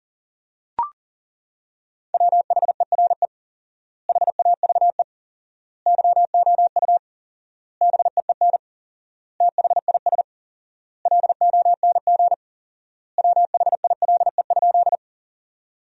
Morse code, heard as a keyed tone, about 40 wpm, 700 Hz. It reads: WHERE HAVE YOU BEEN THIS LONG WHILE?
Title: 4x Word-Spacing - Sentences from Top 300 Words - 40wpm